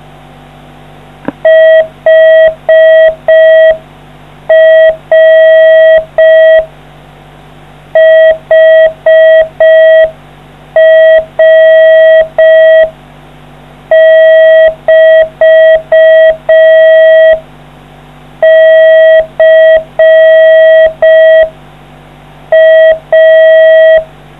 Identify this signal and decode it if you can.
Morse code, heard as a keyed tone, about 4 wpm, 640 Hz. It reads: HR HR <BT> C A